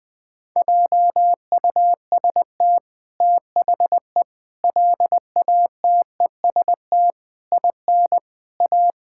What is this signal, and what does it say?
Morse code, heard as a keyed tone, about 20 wpm, 700 Hz. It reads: JUST THE LATEST IN A
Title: Sets of 8 Words - 20wpm